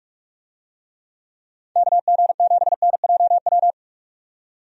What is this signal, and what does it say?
Morse code, heard as a keyed tone, about 44 wpm, 700 Hz. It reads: KG7NJW
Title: USA Call Signs - 6 Characters - 45wpm